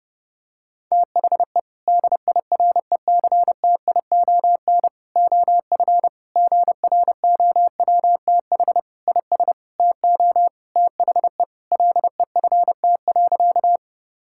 Morse code, heard as a keyed tone, about 30 wpm, 700 Hz. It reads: THE DIRECTION OF GROWTH IS TO THE LEFT.
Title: Sentences from Top 1000 Words - Review Part 5 - 30wpm